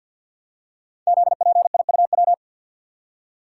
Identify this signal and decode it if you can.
Morse code, heard as a keyed tone, about 54 wpm, 700 Hz. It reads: ZPIUW
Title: Code Groups - 5 Letters - 50wpm